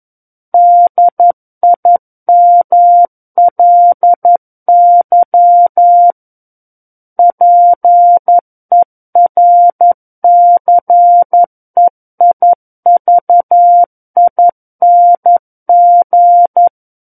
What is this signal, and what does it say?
Morse code, heard as a keyed tone, about 11 wpm, 700 Hz. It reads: DIMLY PERCEIVING
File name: 02-Chapter 2 - 11 WPM